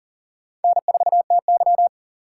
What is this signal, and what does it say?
Morse code, heard as a keyed tone, about 40 wpm, 700 Hz. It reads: N4TY
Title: USA Call Signs - 4x Character Spacing - 40wpm